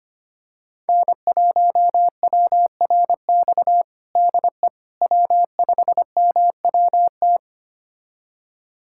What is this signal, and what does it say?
Morse code, heard as a keyed tone, about 25 wpm, 700 Hz. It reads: N1WRX DE W5MWT
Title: USA Callsign DE Callsign - 5 Characters - 25wpm